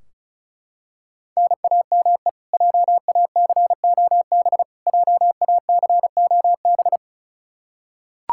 Morse code, heard as a keyed tone, about 36 wpm, 700 Hz. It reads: NAME JACOB JACOB
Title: K1USN SST QSO Practice 35wpm